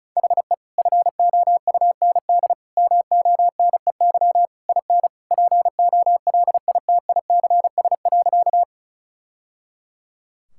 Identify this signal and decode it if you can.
Morse code, heard as a keyed tone, about 36 wpm, 700 Hz. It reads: HE FOUND MONEY IN POLITICS.
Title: Sentences from Top 1000 Words - Review Part 6 - 35wpm